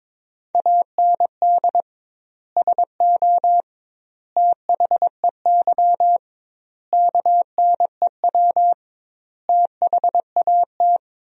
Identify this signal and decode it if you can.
Morse code, heard as a keyed tone, about 22 wpm, 700 Hz. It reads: AND SO THEY KNEW THAT
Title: 2x Word-Spacing - Sentences from Top 400 Words - 22wpm